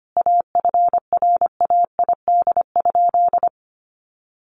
Morse code, heard as a keyed tone, about 25 wpm, 700 Hz. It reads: AFRAID?
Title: Sentences from Top 1500 Words 25wpm